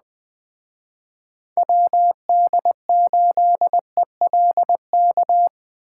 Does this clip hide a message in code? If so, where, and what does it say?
Yes: Morse code from 1.6 to 5.5 s, heard as a keyed tone, about 21 wpm, 700 Hz. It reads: WD8ELK